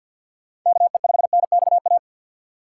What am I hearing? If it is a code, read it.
Morse code, heard as a keyed tone, about 48 wpm, 700 Hz. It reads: KE5NXA
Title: USA Call Signs - 6 Characters - 4x Character Spacing - 50wpm